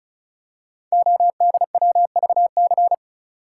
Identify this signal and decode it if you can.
Morse code, heard as a keyed tone, about 36 wpm, 700 Hz. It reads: ODWVC
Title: Code Groups - 5 Letters - 35wpm